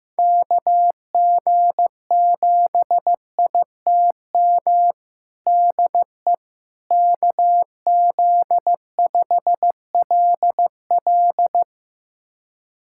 Morse code, heard as a keyed tone, about 16 wpm, 700 Hz. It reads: KG7ITM DE KZ5LL
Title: USA Callsign DE Callsign 15wpm